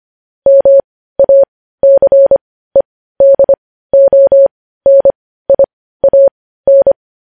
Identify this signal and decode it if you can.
Morse code, heard as a keyed tone, about 25 wpm, spaced as for 15 wpm, 550 Hz. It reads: MACEDONIAN
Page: news-headlines-in-morse-code-at-15-wpm